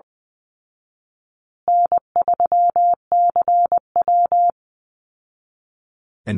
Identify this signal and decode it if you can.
Morse code, heard as a keyed tone, about 20 wpm, 700 Hz. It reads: N3CW
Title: Easy Call Signs Repeated 3x - 20wpm